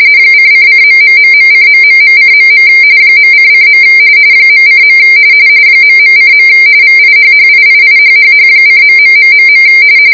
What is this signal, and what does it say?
RTTY, about 45.45 baud. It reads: :::....:...:.. .::. .:. ..:. VIHHIIIHHH:.....